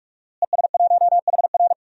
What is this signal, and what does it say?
Morse code, heard as a keyed tone, about 46 wpm, 700 Hz. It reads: ES1HR
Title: DXCC-100 Call Signs - 5 Characters - Repeated 3x - 45wpm